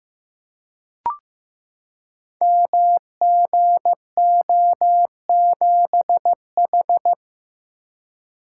Morse code, heard as a keyed tone, about 15 wpm, 700 Hz. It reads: MGO7H
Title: Code Groups - 5 Letter:Numbers - 15wpm